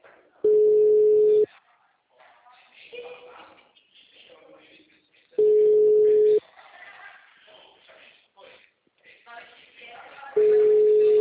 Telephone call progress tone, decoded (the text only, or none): ringback tone